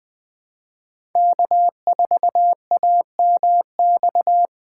Morse code, heard as a keyed tone, about 20 wpm, 700 Hz. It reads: K4AMX
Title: USA Call Signs - 5 Characters - 4x Character Spacing - 20wpm